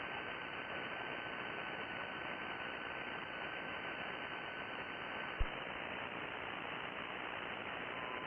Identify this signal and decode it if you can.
FT8.
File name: contests / sept-vhf-k5tr- -2000